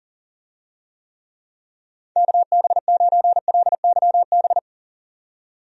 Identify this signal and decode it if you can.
Morse code, heard as a keyed tone, about 40 wpm, 700 Hz. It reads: KB9LYB
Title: USA Call Signs - 6 Characters - 40wpm